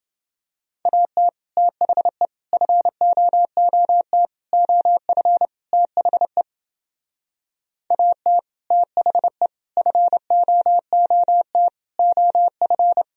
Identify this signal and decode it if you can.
Morse code, heard as a keyed tone, about 29 wpm, 700 Hz. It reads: AT THE FOOT OF THE AT THE FOOT OF